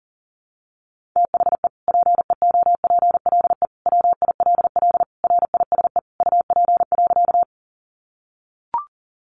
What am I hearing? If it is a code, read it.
Morse code, heard as a keyed tone, about 40 wpm, 700 Hz. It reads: THE PEOPLE WILL RISE UP.
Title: Sentences from Top 900 Words 40wpm